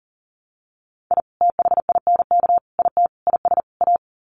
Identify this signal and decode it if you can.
Morse code, heard as a keyed tone, about 37 wpm, 700 Hz. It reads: I THINK IT IS A
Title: Sets of 5 Words Encore 40wpm